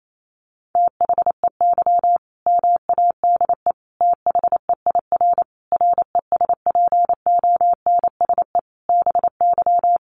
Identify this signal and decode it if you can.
Morse code, heard as a keyed tone, about 28 wpm, 700 Hz. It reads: THEY MADE THEIR RESPONSE BY
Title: Sentences from Top 1400 Words 28wpm